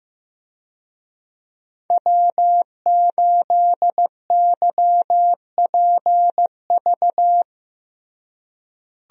Morse code, heard as a keyed tone, about 15 wpm, 700 Hz. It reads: W8YPV
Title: USA Call Signs - 5 Characters - Repeated 2x - 15wpm